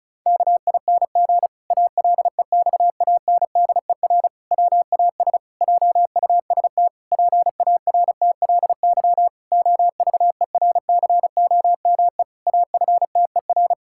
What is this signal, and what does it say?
Morse code, heard as a keyed tone, about 36 wpm, 700 Hz. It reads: KING ALEXANDER WAS JUST PARTLY OVERCOME AFTER
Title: Pangram Challenge v2 - 35wpm